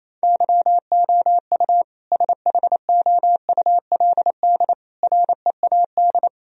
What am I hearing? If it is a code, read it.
Morse code, heard as a keyed tone, about 28 wpm, 700 Hz. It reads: YOU SHOULD READ
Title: Sentences from Top 1000 Words - Review Part 4 - 28wpm